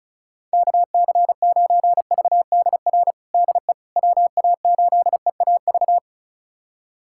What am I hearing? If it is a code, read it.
Morse code, heard as a keyed tone, about 36 wpm, 700 Hz. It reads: KC9VDR DE WA8EAV
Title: USA Callsign DE Callsign 35wpm